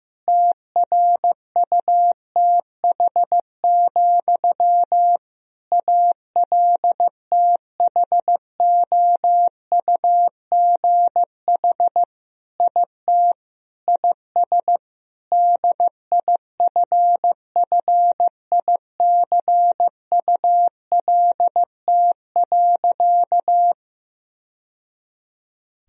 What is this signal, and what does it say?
Morse code, heard as a keyed tone, about 15 wpm, 700 Hz. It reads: TRUTH, ALTHOUGH IT IS DIFFICULT.
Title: Sentences from Top 1000 Words - Review Part 3 - 15wpm